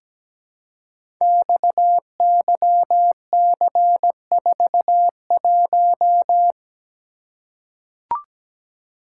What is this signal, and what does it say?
Morse code, heard as a keyed tone, about 17 wpm, 690 Hz. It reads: XYC41
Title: Code Groups - 5 Letter:Numbers - 17wpm